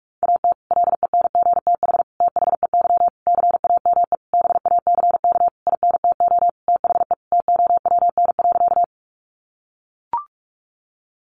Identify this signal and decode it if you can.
Morse code, heard as a keyed tone, about 44 wpm, 700 Hz. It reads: AT LENGTH THEY CAME BACK INTO THE TOWN.
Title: Sentences from Top 1000 Words 45wpm